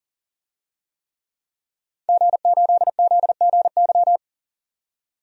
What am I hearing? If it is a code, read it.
Morse code, heard as a keyed tone, about 40 wpm, 700 Hz. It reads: G8ZGY
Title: DXCC-100 Call Signs - 5 Characters - 40wpm